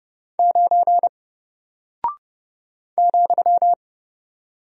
Morse code, heard as a keyed tone, about 30 wpm, 700 Hz. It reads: 9 ,
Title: Single Character (Letter:Number:Punctuation:Prosign) 30wpm